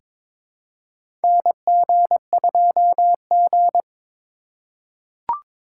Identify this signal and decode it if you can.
Morse code, heard as a keyed tone, about 21 wpm, 700 Hz. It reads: NG2G
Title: USA Call Signs - 4 Characters - 4x Character Spacing - 22wpm